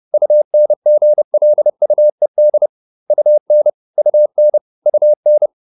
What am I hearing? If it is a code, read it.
Morse code, heard as a keyed tone, about 30 wpm, 600 Hz. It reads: UNGLUED UN UN UN